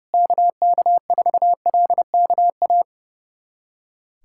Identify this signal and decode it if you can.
Morse code, heard as a keyed tone, about 32 wpm, 700 Hz. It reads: KK4LKA
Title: USA Callsign DE Callsign 30wpm